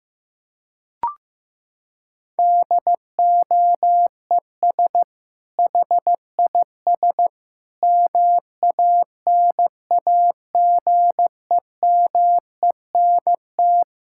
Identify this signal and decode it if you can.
Morse code, heard as a keyed tone, about 15 wpm, 700 Hz. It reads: DOES HIS MANAGEMENT
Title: Sentences from Top 1000 Words - Review Part 6 - 15wpm